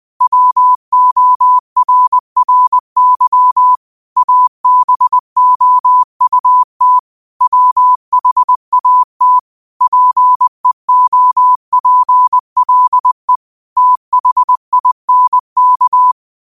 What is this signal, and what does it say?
Morse code, heard as a keyed tone, about 20 wpm, 1000 Hz. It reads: WORRY ABOUT WHAT PEOPLE THINK
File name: qotdfiles / 20wpm